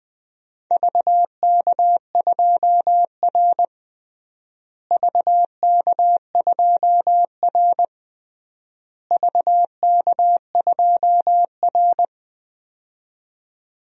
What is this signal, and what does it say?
Morse code, heard as a keyed tone, about 21 wpm, 700 Hz. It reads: VK2R VK2R VK2R
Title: DXCC-100 Call Signs - 4 Characters - Repeated 3x - 20wpm